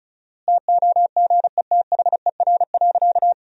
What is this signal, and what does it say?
Morse code, heard as a keyed tone, about 34 wpm, 700 Hz. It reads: TOGETHER.
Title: 4x Word-Spacing - Sentences from Top 400 Words - 35wpm